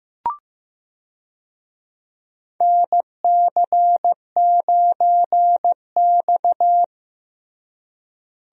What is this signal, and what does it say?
Morse code, heard as a keyed tone, about 15 wpm, 700 Hz. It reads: NC9X
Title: USA Call Signs - 4 Characters - Repeated 3x - 15wpm